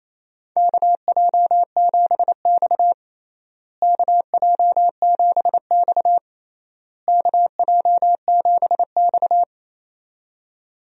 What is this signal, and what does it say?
Morse code, heard as a keyed tone, about 28 wpm, 700 Hz. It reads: KJ7X KJ7X KJ7X
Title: USA Call Signs - 4 Characters - Repeated 3x - 28wpm